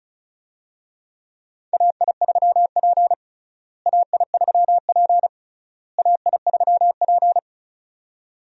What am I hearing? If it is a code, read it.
Morse code, heard as a keyed tone, about 34 wpm, 700 Hz. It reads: AI3P AI3P AI3P
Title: USA Call Signs - 4 Characters - Repeated 3x - 35wpm